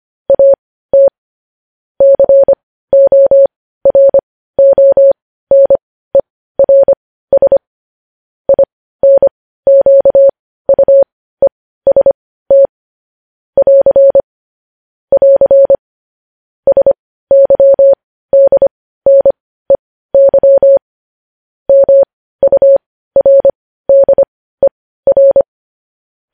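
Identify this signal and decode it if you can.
Morse code, heard as a keyed tone, about 25 wpm, spaced as for 15 wpm, 550 Hz. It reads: AT CORONERS INQUEST <AR> <AR> SYDNEY MURDER
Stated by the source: Morse news headlines practice, possibly sent at 15 wpm.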